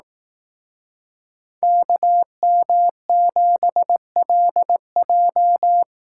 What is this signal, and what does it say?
Morse code, heard as a keyed tone, about 18 wpm, 700 Hz. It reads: KM7LJ
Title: Speed Racing - USA 2x2 Call Signs 15wpm